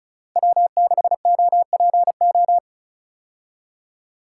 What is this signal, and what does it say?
Morse code, heard as a keyed tone, about 34 wpm, 700 Hz. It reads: W6OPO
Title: K1USN SST QSO Practice - 35wpm